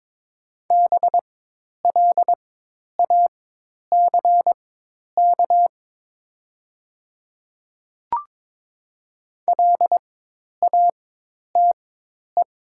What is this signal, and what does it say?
Morse code, heard as a keyed tone, about 22 wpm, spaced as for 10 wpm, 700 Hz. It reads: BLACK LATE
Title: 4x Character Spacing - Top 500 Words - 22wpm